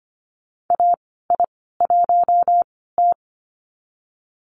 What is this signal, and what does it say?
Morse code, heard as a keyed tone, about 25 wpm, spaced as for 16 wpm, 700 Hz. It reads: AI1T
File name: Lesson 15g - Number 5 - Call Signs 25wpm